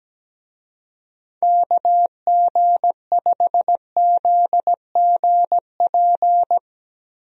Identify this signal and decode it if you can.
Morse code, heard as a keyed tone, about 17 wpm, 700 Hz. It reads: KG5ZGP
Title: USA Call Signs - 6 Characters - Repeated 3x - 17wpm